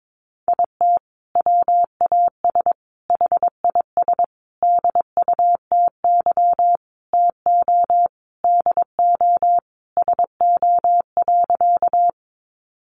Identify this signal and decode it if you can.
Morse code, heard as a keyed tone, about 22 wpm, 700 Hz. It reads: IT WAS HIS DUTY TO DO SO.
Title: Sentences from Top 1500 Words 22wpm